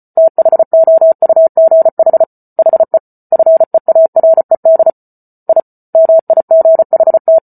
Morse code, heard as a keyed tone, about 34 wpm, 650 Hz. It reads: THOUGH HE FEARED I MIGHT